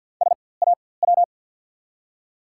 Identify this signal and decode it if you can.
Morse code, heard as a keyed tone, about 54 wpm, spaced as for 24 wpm, 700 Hz. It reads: SAW
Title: 4x Character Spacing - Top 500 Words - 50wpm